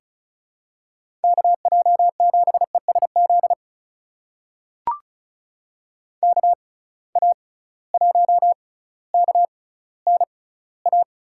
Morse code, heard as a keyed tone, about 34 wpm, spaced as for 18 wpm, 700 Hz. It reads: KJ7ESZ KA1KNA
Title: USA Call Signs - 6 Characters - 6x Character Spacing - 35wpm